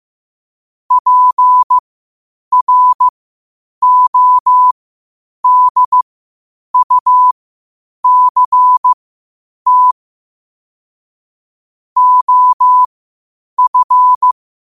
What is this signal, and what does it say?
Morse code, heard as a keyed tone, about 15 wpm, spaced as for 8 wpm, 1000 Hz. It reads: PRODUCT OF
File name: qotdfiles / 8wpm